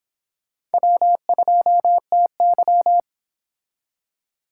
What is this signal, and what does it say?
Morse code, heard as a keyed tone, about 29 wpm, 700 Hz. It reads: W2TY